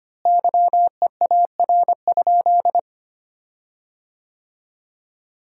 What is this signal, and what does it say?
Morse code, heard as a keyed tone, about 25 wpm, 700 Hz. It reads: YEAR?
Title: 4x Word-Spacing - Sentences from Top 200 Words - 25wpm